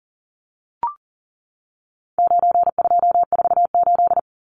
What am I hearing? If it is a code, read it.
Morse code, heard as a keyed tone, about 40 wpm, 700 Hz. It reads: 9248
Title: Numbers - 4-Digit 40wpm